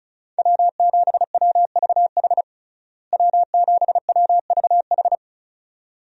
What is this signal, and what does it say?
Morse code, heard as a keyed tone, about 34 wpm, 700 Hz. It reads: W7WVH W7WVH
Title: USA Call Signs - 5 Characters - Repeated 2x - 35wpm